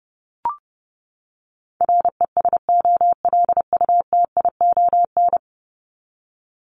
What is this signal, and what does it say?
Morse code, heard as a keyed tone, about 30 wpm, 700 Hz. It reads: RESOLUTION
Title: Top 1501-1750 Words 30wpm